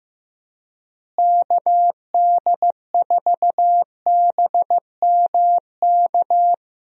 Morse code, heard as a keyed tone, about 15 wpm, 700 Hz. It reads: KD4BMK
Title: USA Call Signs - 6 Characters - Repeated 3x - 15wpm